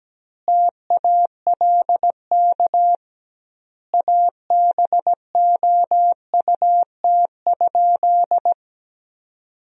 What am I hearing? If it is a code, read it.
Morse code, heard as a keyed tone, about 17 wpm, 700 Hz. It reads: TALK ABOUT?
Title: 2x Word-Spacing - Sentences from Top 400 Words - 17wpm